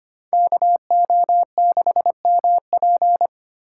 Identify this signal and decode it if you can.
Morse code, heard as a keyed tone, about 32 wpm, 700 Hz. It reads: KO6MP